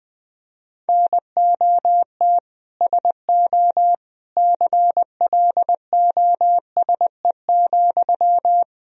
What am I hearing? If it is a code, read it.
Morse code, heard as a keyed tone, about 21 wpm, 700 Hz. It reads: NOT SO CLOSE,
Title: Sentences from Top 1000 Words - Review Part 1 - 20wpm